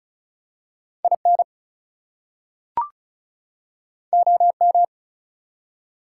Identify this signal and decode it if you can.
Morse code, heard as a keyed tone, about 36 wpm, 700 Hz. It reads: IN OM
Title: Top 180 QSO Elements 35wpm